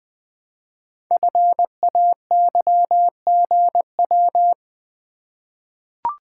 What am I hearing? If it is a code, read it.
Morse code, heard as a keyed tone, about 20 wpm, 700 Hz. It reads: FAYGW
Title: Code Groups - 5 Letters - 20wpm